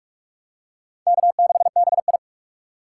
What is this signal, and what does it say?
Morse code, heard as a keyed tone, about 46 wpm, 700 Hz. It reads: K6BI
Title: USA Call Signs - 4 Characters - 3x Character Spacing - 45wpm